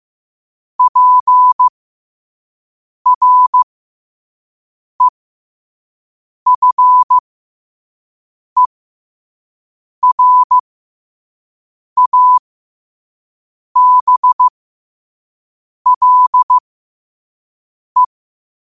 Morse code, heard as a keyed tone, about 15 wpm, spaced as for 5 wpm, 1000 Hz. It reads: PREFERABLE
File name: qotdfiles / 5wpm